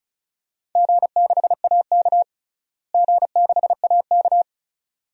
Morse code, heard as a keyed tone, about 36 wpm, 700 Hz. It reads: G6AK G6AK
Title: DXCC-100 Call Signs - 4 Characters - Repeated 2x - 35wpm